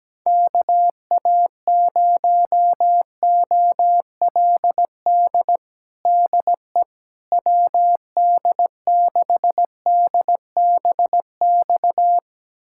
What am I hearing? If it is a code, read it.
Morse code, heard as a keyed tone, about 17 wpm, 700 Hz. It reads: KA0OLD DE WD6DBX
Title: USA Callsign DE Callsign - 6 Characters - 17wpm